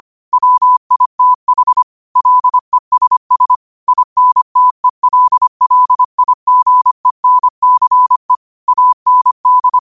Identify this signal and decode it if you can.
Morse code, heard as a keyed tone, about 25 wpm, 1000 Hz. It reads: WITH LESS INTELLIGENCE AND